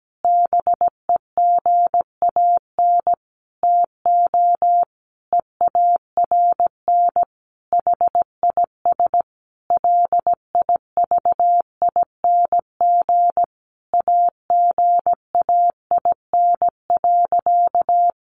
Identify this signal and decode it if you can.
Morse code, heard as a keyed tone, about 17 wpm, 700 Hz. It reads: BEGAN TO EARN HIS LIVING AGAIN.
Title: Sentences from Top 1500 Words 17wpm